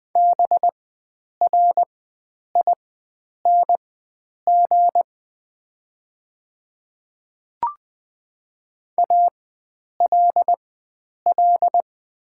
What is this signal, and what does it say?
Morse code, heard as a keyed tone, about 20 wpm, spaced as for 9 wpm, 690 Hz. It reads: BRING ALL
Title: 4x Character Spacing - Top 500 Words - 20wpm